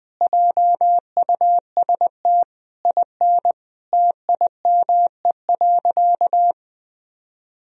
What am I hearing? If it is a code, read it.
Morse code, heard as a keyed tone, about 20 wpm, 700 Hz. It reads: JUST IN TIME.
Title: Sentences from Top 1000 Words - Review Part 2 - 20wpm